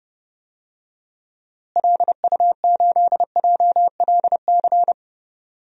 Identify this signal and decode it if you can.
Morse code, heard as a keyed tone, about 30 wpm, 700 Hz. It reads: LU8JLC
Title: DXCC-100 Call Signs - 6 Characters - Repeated 2x - 30wpm